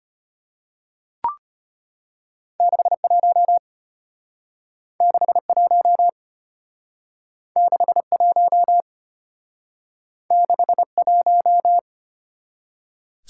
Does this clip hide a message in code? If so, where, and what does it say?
Yes: Morse code from 2.6 to 11.8 s, heard as a keyed tone, about 34 wpm, 700 Hz. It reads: 61 61 61 61